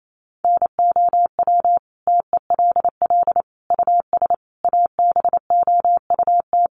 Morse code, heard as a keyed tone, about 28 wpm, 700 Hz. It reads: NOW TELL US ABOUT
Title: Sentences from Top 600 Words 28wpm